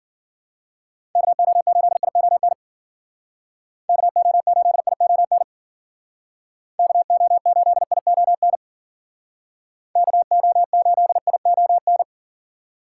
Morse code, heard as a keyed tone, about 51 wpm, 700 Hz. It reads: KO8ION KO8ION KO8ION KO8ION